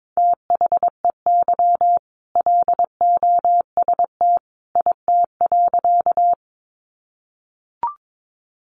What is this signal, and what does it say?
Morse code, heard as a keyed tone, about 22 wpm, 700 Hz. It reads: THEY LOST IT.
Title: Sentences from Top 1400 Words 22wpm